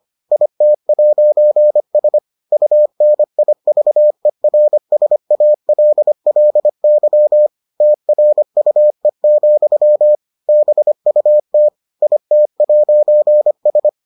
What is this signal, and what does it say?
Morse code, heard as a keyed tone, about 25 wpm, 600 Hz. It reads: IT'S UNIVERSALLY TRUE, BUT IT'S